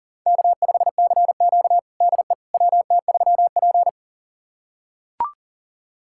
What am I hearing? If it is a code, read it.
Morse code, heard as a keyed tone, about 40 wpm, 700 Hz. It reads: K5CQ DE WT3P
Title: USA Callsign DE Callsign - 4 Characters - 40wpm